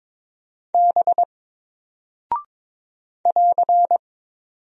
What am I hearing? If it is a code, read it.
Morse code, heard as a keyed tone, about 22 wpm, 700 Hz. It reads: B <AR>
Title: Single Character (Letter:Number:Punctuation:Prosign) 22wpm